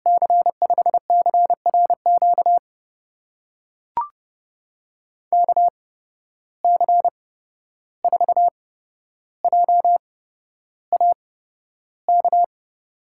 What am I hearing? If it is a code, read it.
Morse code, heard as a keyed tone, about 31 wpm, spaced as for 12 wpm, 700 Hz. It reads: C5CRQ KC4JAK
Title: USA Call Signs - 6 Characters - 8x Character Spacing - 30wpm